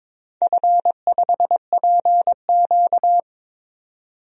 Morse code, heard as a keyed tone, about 22 wpm, 700 Hz. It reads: F5PQ
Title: DXCC-100 Call Signs - 4 Characters - 22wpm